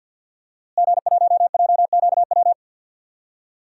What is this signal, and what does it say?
Morse code, heard as a keyed tone, about 49 wpm, 700 Hz. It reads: G1JQW
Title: DXCC-100 Call Signs - 5 Characters - Repeated 3x - 50wpm